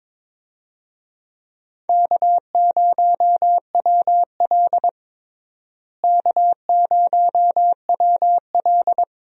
Morse code, heard as a keyed tone, about 22 wpm, 700 Hz. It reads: K0WL K0WL
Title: USA Call Signs - 4 Characters - Repeated 3x - 22wpm